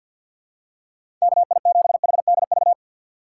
Morse code, heard as a keyed tone, about 50 wpm, 700 Hz. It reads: KI7HDV
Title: USA Call Signs - 6 Characters - 3x Character Spacing - 50wpm